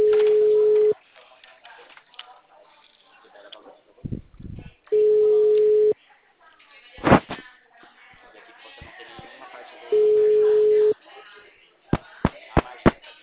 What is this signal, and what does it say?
Ringback tone (Europe, 425 Hz), with speech.
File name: out- -2015- - - .73623